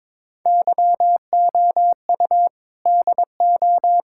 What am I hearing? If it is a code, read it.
Morse code, heard as a keyed tone, about 22 wpm, 700 Hz. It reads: YOU DO
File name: practice / 22wpm / Build A Phrase - 5-Word Phrases - 22wpm